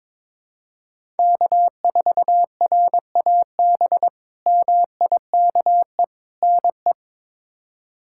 Morse code, heard as a keyed tone, about 21 wpm, 700 Hz. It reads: K4RAB MIKE NE
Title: K1USN SST QSO Practice - 22wpm